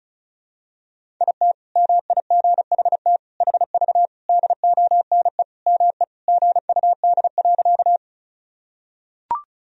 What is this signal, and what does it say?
Morse code, heard as a keyed tone, about 36 wpm, 700 Hz. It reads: IT MIGHT HV DONE ME GUD.
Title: Sentences from Top 500 Words and Abbreviations 35wpm